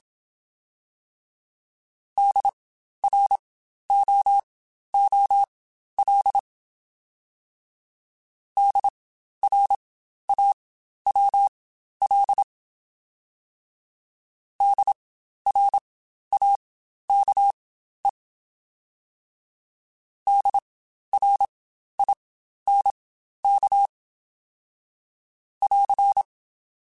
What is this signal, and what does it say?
Morse code, heard as a keyed tone, about 27 wpm, spaced as for 12 wpm, 800 Hz. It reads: DROOL DRAWL DRAKE DRINK <AR>
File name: Spaced Alphas / Spaced Alphas mp3s / Ds sp @25